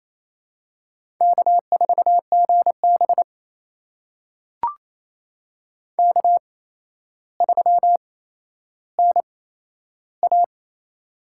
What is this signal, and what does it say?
Morse code, heard as a keyed tone, about 27 wpm, spaced as for 12 wpm, 700 Hz. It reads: K4GB K3NA
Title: USA Call Signs - 8x Character Spacing - 28wpm